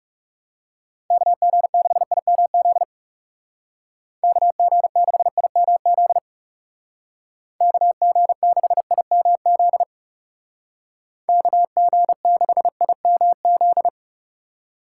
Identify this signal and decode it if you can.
Morse code, heard as a keyed tone, about 37 wpm, 700 Hz. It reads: KG6IMZ KG6IMZ KG6IMZ KG6IMZ
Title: Speed Racing - USA 2x3 Call Signs 30wpm